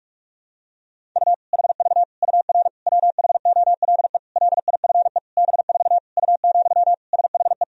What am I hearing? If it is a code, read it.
Morse code, heard as a keyed tone, about 46 wpm, 700 Hz. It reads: U HV UR WHOLE LIFE B4 U, SHE
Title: Sentences from Top 400 Words and Abbreviations 45wpm